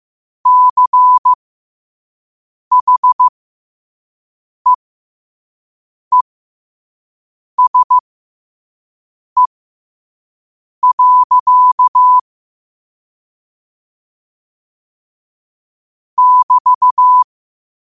Morse code, heard as a keyed tone, about 15 wpm, spaced as for 5 wpm, 1000 Hz. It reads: CHEESE. <BT>
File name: qotdfiles / 5wpm